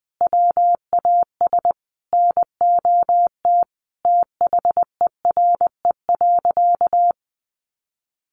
Morse code, heard as a keyed tone, about 19 wpm, 700 Hz. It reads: WAS NOT THERE.
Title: Sentences from Top 1400 Words 20wpm